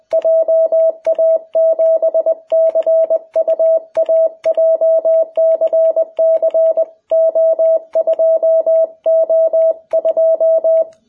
Morse code, heard as a keyed tone, about 20 wpm, 660 Hz. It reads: JA7CUA JCC O2O2
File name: jh8cbh / 2021cw